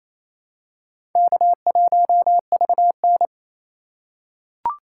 Morse code, heard as a keyed tone, about 28 wpm, 700 Hz. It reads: K1VN
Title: USA Call Signs - 4 Characters - 2x Character Spacing - 28wpm